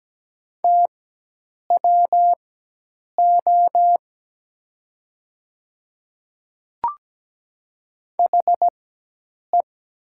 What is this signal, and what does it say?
Morse code, heard as a keyed tone, about 17 wpm, spaced as for 8 wpm, 700 Hz. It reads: TWO HE
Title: 4x Character Spacing - Top 100 Words - 17wpm